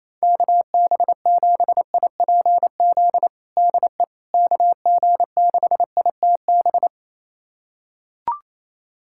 Morse code, heard as a keyed tone, about 28 wpm, 700 Hz. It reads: KB7IPZ DE KG6ITB
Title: USA Callsign DE Callsign 28wpm